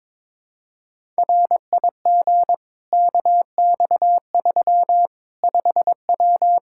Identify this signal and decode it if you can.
Morse code, heard as a keyed tone, about 22 wpm, 700 Hz. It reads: RIG KX3 5W
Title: Practice QSO - Exchange 2 - Info - 22wpm